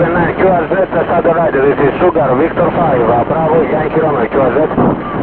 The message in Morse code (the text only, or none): none